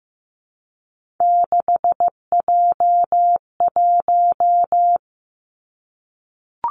Morse code, heard as a keyed tone, about 15 wpm, 700 Hz. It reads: 6J1
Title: Three Challenging Letter-Number Combinations 15wpm